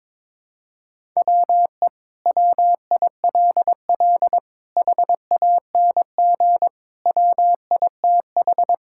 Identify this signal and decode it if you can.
Morse code, heard as a keyed tone, about 22 wpm, 700 Hz. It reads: WE WILL HANG WITH
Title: Sentences from Top 1600 Words 22wpm